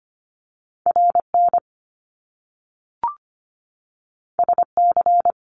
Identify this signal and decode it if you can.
Morse code, heard as a keyed tone, about 25 wpm, 700 Hz. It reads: RN SC
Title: Top Two Letter Combinations in Words 25wpm